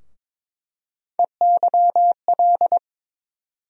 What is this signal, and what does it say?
Morse code, heard as a keyed tone, about 22 wpm, 700 Hz. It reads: EYL
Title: USA Call Sign Postfixes - 3 Letters 22wpm